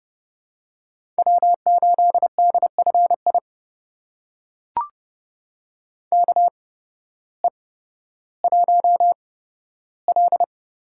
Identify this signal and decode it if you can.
Morse code, heard as a keyed tone, about 30 wpm, spaced as for 14 wpm, 700 Hz. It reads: W8DFI KE1L
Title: USA Call Signs - 5 Characters - 8x Character Spacing - 30wpm